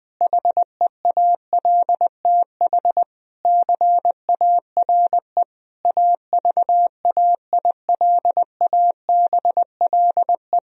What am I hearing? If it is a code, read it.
Morse code, heard as a keyed tone, about 20 wpm, 700 Hz. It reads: HEALTH CARE AVAILABLE
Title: Sentences from Top 1000 Words - Review Part 3 - 20wpm